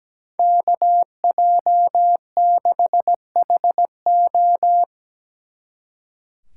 Morse code, heard as a keyed tone, about 19 wpm, 700 Hz. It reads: KJ6HO
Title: Speed Racing - USA 2x2 Call Signs 17wpm